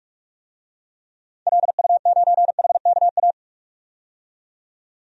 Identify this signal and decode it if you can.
Morse code, heard as a keyed tone, about 41 wpm, 700 Hz. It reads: LU9HKA
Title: DXCC-100 Call Signs - 6 Characters - 45wpm